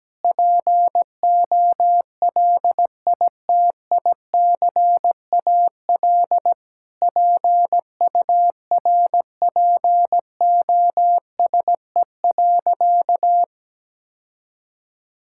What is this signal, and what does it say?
Morse code, heard as a keyed tone, about 17 wpm, 700 Hz. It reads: POLITICAL PURPOSE.
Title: Sentences from Top 1000 Words - Review Part 5 - 17wpm